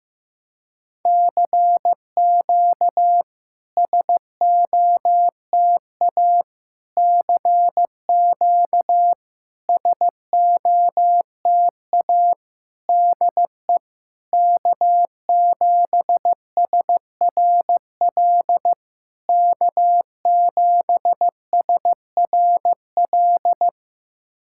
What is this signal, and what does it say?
Morse code, heard as a keyed tone, about 15 wpm, 690 Hz. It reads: CQ SOTA CQ SOTA DE K7SRL K7SRL
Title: SOTA QSO Practice 15wpm